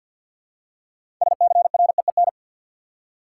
Morse code, heard as a keyed tone, about 52 wpm, 700 Hz. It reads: SCREEN
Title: Six Letter Words - 50wpm